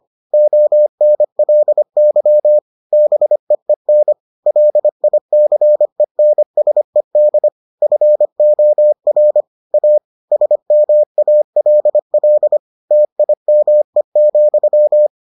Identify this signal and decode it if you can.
Morse code, heard as a keyed tone, about 25 wpm, 600 Hz. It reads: ONLY BEEN LICENSED FOR A SMALL TIME,